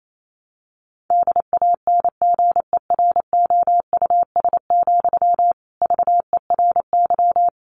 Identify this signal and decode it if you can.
Morse code, heard as a keyed tone, about 28 wpm, 700 Hz. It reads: DANGEROUS, VERY
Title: Sentences from Top 1200 Words 28wpm